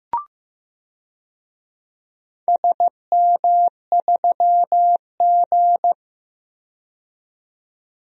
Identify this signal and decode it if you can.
Morse code, heard as a keyed tone, about 15 wpm, 700 Hz. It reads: SM3G
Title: DXCC-100 Call Signs - 4 Characters - 15wpm